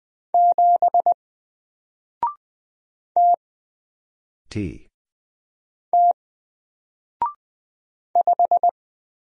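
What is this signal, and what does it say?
Morse code, heard as a keyed tone, about 20 wpm, 700 Hz. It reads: 7 T T 5
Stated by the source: Single letter and number Morse drill at 20 wpm, from N4LCD.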